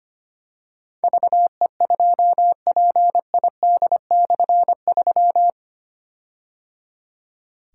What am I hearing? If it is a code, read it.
Morse code, heard as a keyed tone, about 25 wpm, 700 Hz. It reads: VE2PID/3
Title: K1USN SST QSO Practice - 25wpm